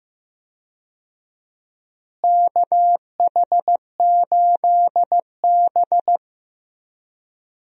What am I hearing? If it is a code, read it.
Morse code, heard as a keyed tone, about 15 wpm, 700 Hz. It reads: KH8B
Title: USA Call Signs - 4 Characters - Repeated 2x - 15wpm